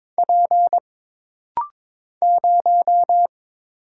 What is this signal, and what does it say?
Morse code, heard as a keyed tone, about 22 wpm, 700 Hz. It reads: P 0
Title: Single Letter-Number - Rapid-Fire 22wpm